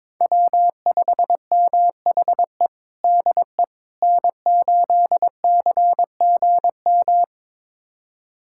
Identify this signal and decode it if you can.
Morse code, heard as a keyed tone, about 22 wpm, 700 Hz. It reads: W5MHE DE N8CGM
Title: USA Callsign DE Callsign - 5 Characters - 22wpm